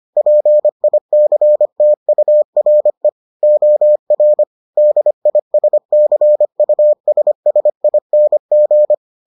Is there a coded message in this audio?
Yes: Morse code from 0.2 to 8.9 s, heard as a keyed tone, about 25 wpm, 600 Hz. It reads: PICTURE OR DISCUSSING